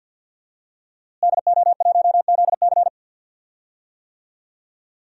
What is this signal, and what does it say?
Morse code, heard as a keyed tone, about 51 wpm, 700 Hz. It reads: DO1ZC
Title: DXCC-100 Call Signs - 5 Characters - Repeated 3x - 50wpm